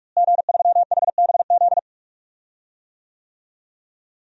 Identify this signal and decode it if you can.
Morse code, heard as a keyed tone, about 46 wpm, 700 Hz. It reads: G3HBZ
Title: DXCC-100 Call Signs - 5 Characters - Repeated 2x - 45wpm